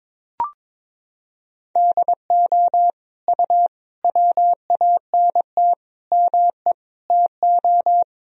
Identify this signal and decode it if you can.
Morse code, heard as a keyed tone, about 22 wpm, 700 Hz. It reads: DO U WANT ME TO
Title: Sentences from Top 200 Words and Abbreviations 22wpm